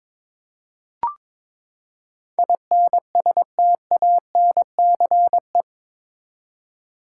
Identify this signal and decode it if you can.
Morse code, heard as a keyed tone, about 22 wpm, 700 Hz. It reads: INSTANCE
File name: practice / 22wpm / Eight Letter Words - 22wpm